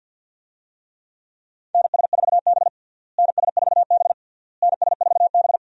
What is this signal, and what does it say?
Morse code, heard as a keyed tone, about 52 wpm, 700 Hz. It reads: NS4B NS4B NS4B
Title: USA Call Signs - 4 Characters - Repeated 3x - 50wpm